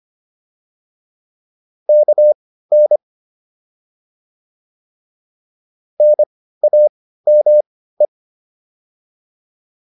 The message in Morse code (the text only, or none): KN NAME